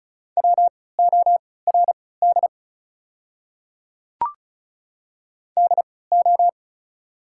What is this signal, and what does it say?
Morse code, heard as a keyed tone, about 36 wpm, spaced as for 20 wpm, 690 Hz. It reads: WORD DO
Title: 3x Character Spacing - Top 100 Words - 35wpm